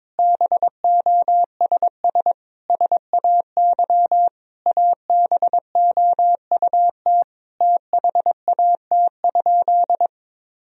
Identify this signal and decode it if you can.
Morse code, heard as a keyed tone, about 22 wpm, 700 Hz. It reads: BOSS SAY ABOUT THAT?
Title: Sentences from Top 1000 Words - Review Part 4 - 22wpm